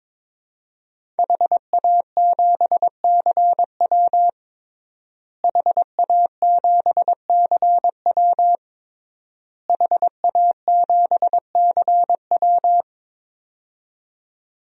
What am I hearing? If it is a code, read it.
Morse code, heard as a keyed tone, about 22 wpm, 700 Hz. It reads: HA7CW HA7CW HA7CW
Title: DXCC-100 Call Signs - 5 Characters - Repeated 3x - 22wpm